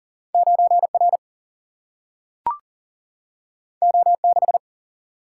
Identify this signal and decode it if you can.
Morse code, heard as a keyed tone, about 39 wpm, 700 Hz. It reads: 9R O6
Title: Single-Letter Single-Number 40wpm